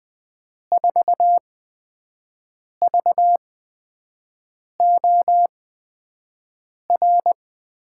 Morse code, heard as a keyed tone, about 20 wpm, spaced as for 5 wpm, 700 Hz. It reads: 4VOR
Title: USA Call Signs - 6 Characters - 8x Character Spacing - 20wpm